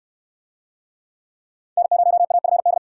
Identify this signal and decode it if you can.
Morse code, heard as a keyed tone, about 66 wpm, 700 Hz. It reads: N0AFD